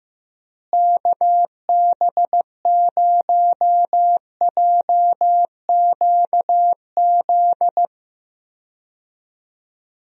Morse code, heard as a keyed tone, about 15 wpm, 700 Hz. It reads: KB0JQZ
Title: USA Call Signs - 6 Characters - Repeated 3x - 15wpm